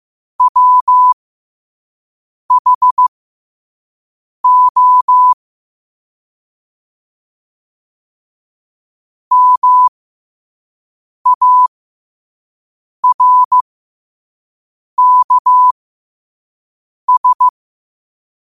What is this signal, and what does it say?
Morse code, heard as a keyed tone, about 15 wpm, spaced as for 5 wpm, 1000 Hz. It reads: WHO MARKS
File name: qotdfiles / 5wpm